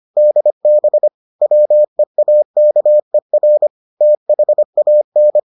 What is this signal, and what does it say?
Morse code, heard as a keyed tone, about 25 wpm, 600 Hz. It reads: DB WEAKER THAN